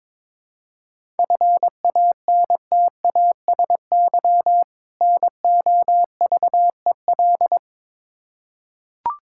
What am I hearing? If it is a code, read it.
Morse code, heard as a keyed tone, about 22 wpm, 700 Hz. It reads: FANTASY NOVEL
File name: Difficult Words with Hints - 22wpm